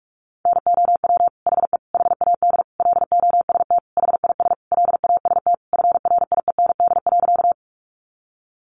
Morse code, heard as a keyed tone, about 44 wpm, 700 Hz. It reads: NOW HE HAD LOST HIS LAST FRIEND.
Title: Sentences from Top 1400 Words 45wpm